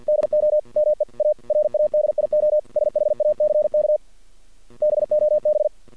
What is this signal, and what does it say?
Morse code, heard as a keyed tone, about 48 wpm, 600 Hz. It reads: DO ME A RADIO SHACK BOB
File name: archived / CRSnet